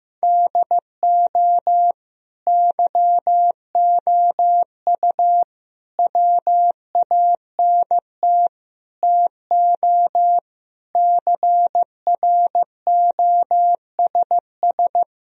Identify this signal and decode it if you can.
Morse code, heard as a keyed tone, about 15 wpm, 700 Hz. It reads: DO YOU WANT TO CROSS
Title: Sentences from Top 1000 Words - Review Part 1 - 15wpm